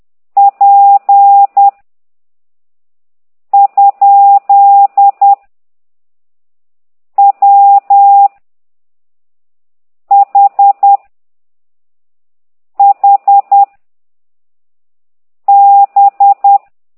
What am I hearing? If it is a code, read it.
Morse code, heard as a keyed tone, about 10 wpm, 800 Hz. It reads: P ? W H H B